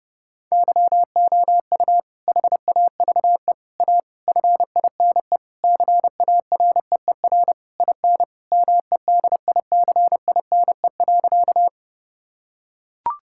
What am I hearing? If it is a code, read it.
Morse code, heard as a keyed tone, about 32 wpm, 700 Hz. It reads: YOU HAVE A FINE CAREER IN MEDICINE.
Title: Sentences from Top 1000 Words - Review Part 6 - 30wpm